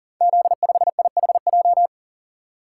Morse code, heard as a keyed tone, about 40 wpm, 700 Hz. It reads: Z5IHJ
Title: Code Groups - 5 Letter:Numbers - 40wpm